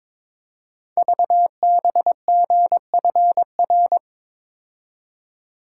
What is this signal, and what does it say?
Morse code, heard as a keyed tone, about 23 wpm, 700 Hz. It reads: VBGFR
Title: Code Groups - 5 Letters - 22wpm